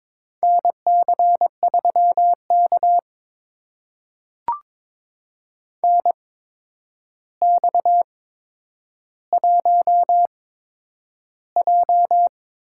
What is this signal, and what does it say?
Morse code, heard as a keyed tone, about 22 wpm, spaced as for 9 wpm, 700 Hz. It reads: NC3K NX1J
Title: USA Call Signs - 4 Characters - 8x Character Spacing - 22wpm